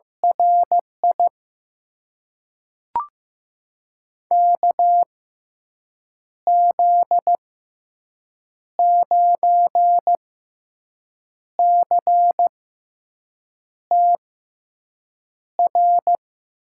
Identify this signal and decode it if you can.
Morse code, heard as a keyed tone, about 15 wpm, spaced as for 6 wpm, 700 Hz. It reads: RI KZ9CTR
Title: USA Call Signs - 6 Characters - 6x Character Spacing - 15wpm